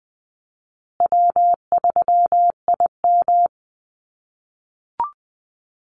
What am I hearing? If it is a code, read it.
Morse code, heard as a keyed tone, about 20 wpm, 700 Hz. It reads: W3IM
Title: USA Call Signs Repeated 2x 20wpm